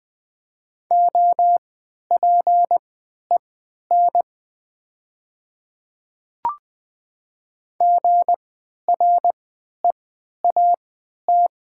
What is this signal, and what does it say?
Morse code, heard as a keyed tone, about 20 wpm, spaced as for 11 wpm, 700 Hz. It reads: OPEN GREAT
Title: 3x Character Spacing - Top 300 Words - 20wpm